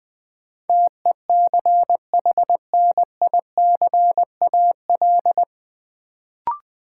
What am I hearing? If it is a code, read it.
Morse code, heard as a keyed tone, about 21 wpm, 700 Hz. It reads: TECHNICAL
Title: Sentences from Top 1100 Words - New Word Drill - 20wpm